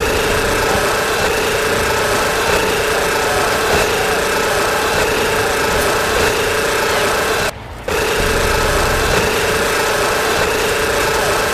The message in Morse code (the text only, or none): none